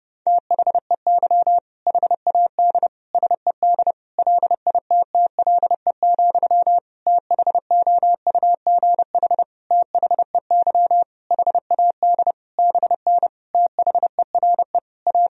Morse code, heard as a keyed tone, about 30 wpm, 700 Hz. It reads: THEY HAD SED LITTLE, THOUGH THEY HAD BN THERE A
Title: Sentences from Top 400 Words and Abbreviations 30wpm